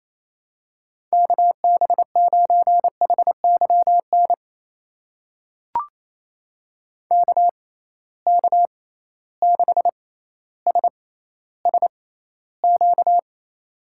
Morse code, heard as a keyed tone, about 28 wpm, spaced as for 14 wpm, 700 Hz. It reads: KB9HYN KK6SSQ
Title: USA Call Signs - 6 Characters - 6x Character Spacing - 28wpm